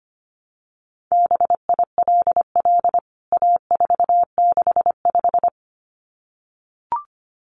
Morse code, heard as a keyed tone, about 25 wpm, 700 Hz. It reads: BILL A465
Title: CWT Exchange 25wpm